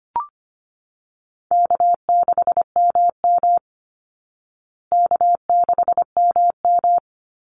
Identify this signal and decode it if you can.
Morse code, heard as a keyed tone, about 25 wpm, 700 Hz. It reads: K6MM K6MM
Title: Call Signs Repeated 2x - 25wpm